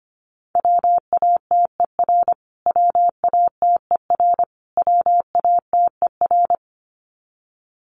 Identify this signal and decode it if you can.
Morse code, heard as a keyed tone, about 25 wpm, 700 Hz. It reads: WATER WATER WATER
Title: Top 200 Words Repeated 25wpm